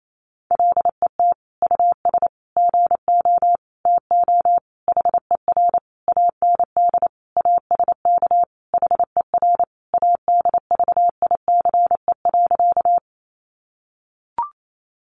Morse code, heard as a keyed tone, about 28 wpm, 700 Hz. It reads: LET US GO TO HER AND ASK HER ADVICE.
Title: Sentences from Top 1100 Words 28wpm